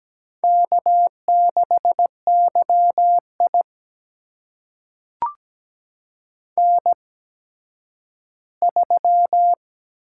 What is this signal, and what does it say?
Morse code, heard as a keyed tone, about 17 wpm, spaced as for 10 wpm, 700 Hz. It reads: K6YI N3
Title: USA Call Signs - 8x Character Spacing - 17wpm